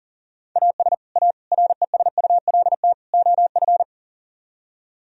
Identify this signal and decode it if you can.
Morse code, heard as a keyed tone, about 35 wpm, 700 Hz. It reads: AS A RESULT OF
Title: Speed-Racing - Sets of 4 Words 40wpm